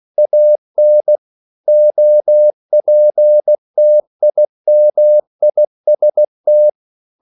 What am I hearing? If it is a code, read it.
Morse code, heard as a keyed tone, about 16 wpm, 600 Hz. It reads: AN OPTIMIST
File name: CW / Practice / EnglishQuotations0416